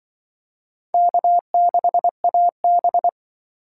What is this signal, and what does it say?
Morse code, heard as a keyed tone, about 24 wpm, 700 Hz. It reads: K6AB
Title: Speed Racing - USA Call Signs 20wpm